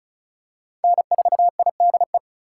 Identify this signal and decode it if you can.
Morse code, heard as a keyed tone, about 34 wpm, 700 Hz. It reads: N4IDE
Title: USA Call Signs - 5 Characters - 4x Character Spacing - 35wpm